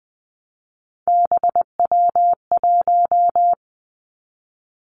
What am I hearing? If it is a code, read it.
Morse code, heard as a keyed tone, about 20 wpm, 700 Hz. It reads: BW1
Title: Three Challenging Letter-Number Combinations 20wpm